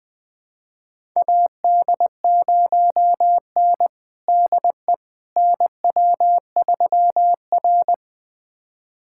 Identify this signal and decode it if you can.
Morse code, heard as a keyed tone, about 20 wpm, 700 Hz. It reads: AD0N DE NW3R
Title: USA Callsign DE Callsign - 4 Characters - 20wpm